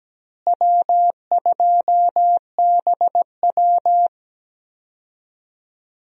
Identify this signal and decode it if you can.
Morse code, heard as a keyed tone, about 17 wpm, 700 Hz. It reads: W2BW
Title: USA Call Signs - 4 Characters - Repeated 3x - 17wpm